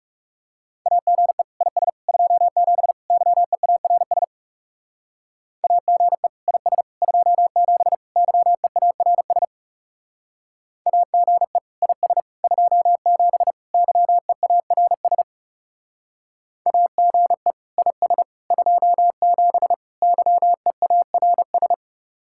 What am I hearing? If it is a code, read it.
Morse code, heard as a keyed tone, about 37 wpm, 700 Hz. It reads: AGE IS 27 YEARS AGE IS 27 YEARS AGE IS 27 YEARS AGE IS 27 YEARS
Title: Speed-Racing Common QSO Phrases Encore - 30wpm